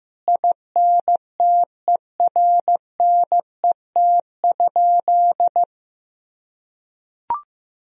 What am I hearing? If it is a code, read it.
Morse code, heard as a keyed tone, about 15 wpm, 700 Hz. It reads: INTERNET?
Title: Sentences from Top 1000 Words - Review Part 4 - 15wpm